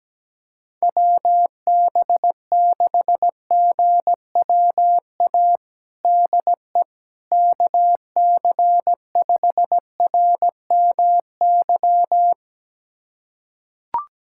Morse code, heard as a keyed tone, about 17 wpm, 700 Hz. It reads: WB6GWA DE KC5RMY
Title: USA Callsign DE Callsign 17wpm